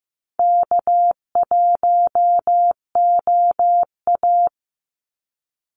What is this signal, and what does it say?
Morse code, heard as a keyed tone, about 15 wpm, 700 Hz. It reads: K1OA
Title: Call Signs Repeated 2x - 15wpm